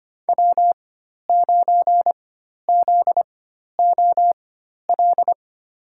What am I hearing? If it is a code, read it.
Morse code, heard as a keyed tone, about 25 wpm, spaced as for 12 wpm, 700 Hz. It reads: W9ZOL
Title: USA Call Signs - 5 Characters - 4x Character Spacing - 25wpm